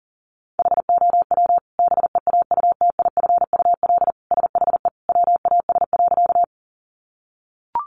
Morse code, heard as a keyed tone, about 40 wpm, 700 Hz. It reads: HOW BEAUTIFUL SHE WAS.
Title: Sentences from Top 900 Words 40wpm